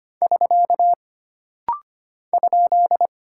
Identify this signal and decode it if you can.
Morse code, heard as a keyed tone, about 25 wpm, 700 Hz. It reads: <SK> ?
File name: Single Punctuation-Prosign - Rapid-Fire - 25wpm